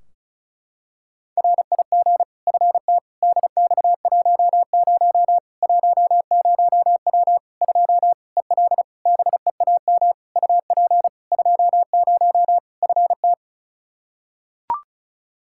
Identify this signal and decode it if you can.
Morse code, heard as a keyed tone, about 36 wpm, 700 Hz. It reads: RIG FT DX10 10W 2 EL BEAM UP 20 FT
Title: Practice QSO - Exchange 2 - Info - 35wpm